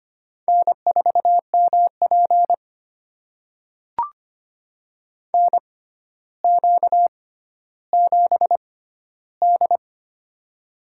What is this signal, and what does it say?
Morse code, heard as a keyed tone, about 25 wpm, spaced as for 13 wpm, 700 Hz. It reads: N4MP NQ7D
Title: USA Call Signs - 4 Characters - 6x Character Spacing - 25wpm